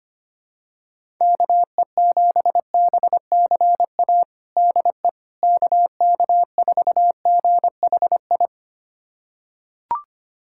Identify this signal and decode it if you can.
Morse code, heard as a keyed tone, about 25 wpm, 700 Hz. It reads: KE7BCA DE KK4GHI
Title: USA Callsign DE Callsign 25wpm